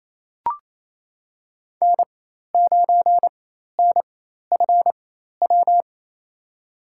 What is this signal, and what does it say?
Morse code, heard as a keyed tone, about 28 wpm, spaced as for 13 wpm, 700 Hz. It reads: N9NFW
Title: USA Call Signs - 4x Character Spacing - 28wpm